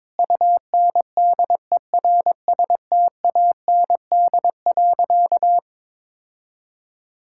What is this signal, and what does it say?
Morse code, heard as a keyed tone, about 22 wpm, 700 Hz. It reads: UNDERSTAND.
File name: practice / 22wpm / Sentences with 4-Word Phrases - 22wpm